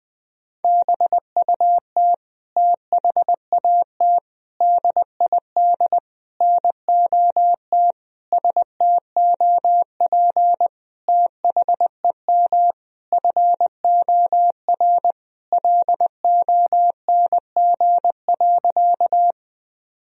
Morse code, heard as a keyed tone, about 21 wpm, 700 Hz. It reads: BUT THAT DID NOT STOP THEM FOR LONG.
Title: Sentences from Top 1000 Words - Review Part 1 - 20wpm